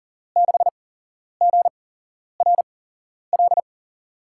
Morse code, heard as a keyed tone, about 40 wpm, spaced as for 11 wpm, 700 Hz. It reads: 6GRL
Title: USA Call Signs - 8x Character Spacing - 40wpm